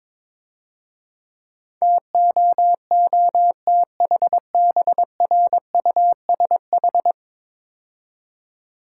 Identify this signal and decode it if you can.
Morse code, heard as a keyed tone, about 22 wpm, 700 Hz. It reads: TOOTHBRUSH
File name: Easy Words - Speed-Racing with Hints - 22wpm